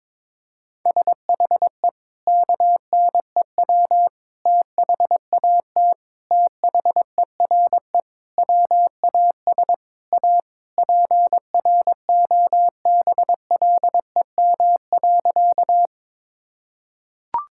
Morse code, heard as a keyed tone, about 22 wpm, 700 Hz. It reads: SHE KNEW THAT THERE WAS A PROBLEM.
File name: practice / 22wpm / Sentences with 4-Word Phrases - 22wpm